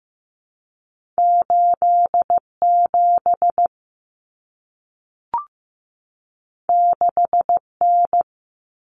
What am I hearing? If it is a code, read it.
Morse code, heard as a keyed tone, about 15 wpm, 700 Hz. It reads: 87 6N
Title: Common Cut Numbers - 2-Digit 15wpm